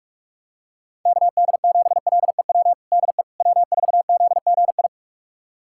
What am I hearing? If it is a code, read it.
Morse code, heard as a keyed tone, about 46 wpm, 700 Hz. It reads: KD7LEW DE W4ZGI
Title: USA Callsign DE Callsign 45wpm